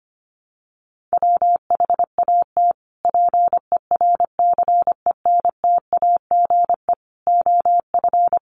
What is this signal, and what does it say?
Morse code, heard as a keyed tone, about 25 wpm, 700 Hz. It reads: WHAT PERCENTAGE OF
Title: Sentences from Top 1300 Words 25wpm